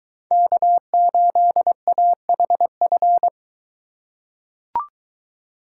Morse code, heard as a keyed tone, about 21 wpm, 700 Hz. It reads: K8AHF